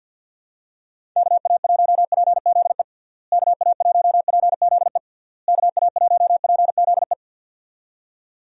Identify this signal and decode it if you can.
Morse code, heard as a keyed tone, about 50 wpm, 700 Hz. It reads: KA1PZE KA1PZE KA1PZE
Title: USA Call Signs - 6 Characters - Repeated 3x - 50wpm